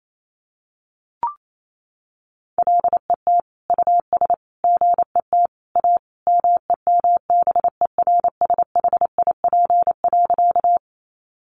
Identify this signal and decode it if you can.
Morse code, heard as a keyed tone, about 28 wpm, 700 Hz. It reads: LET US GET A MEMBERSHIP.
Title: Sentences from Top 1400 Words 28wpm